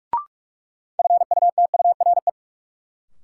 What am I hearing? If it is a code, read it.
Morse code, heard as a keyed tone, about 42 wpm, 700 Hz. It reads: FUTURE
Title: Sets of 3 Words - New Word Drill - 45wpm